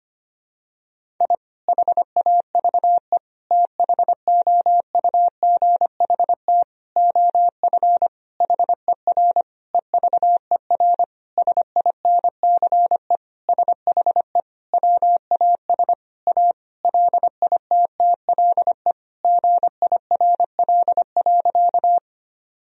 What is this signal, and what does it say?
Morse code, heard as a keyed tone, about 25 wpm, 700 Hz. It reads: I HAVE THOUGHT OF HER EVER SINCE SHE WAS A LITTLE GIRL.
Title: Sentences from Top 1000 Words - Review Part 1 - 25wpm